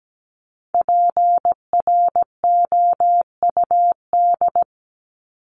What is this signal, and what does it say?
Morse code, heard as a keyed tone, about 17 wpm, 700 Hz. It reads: PROUD
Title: Top 1501-1750 Words 17wpm